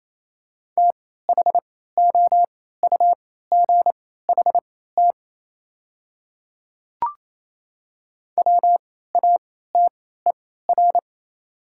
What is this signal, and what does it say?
Morse code, heard as a keyed tone, about 27 wpm, spaced as for 16 wpm, 690 Hz. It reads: THOUGHT WATER
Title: 3x Character Spacing - Top 300 Words - 28wpm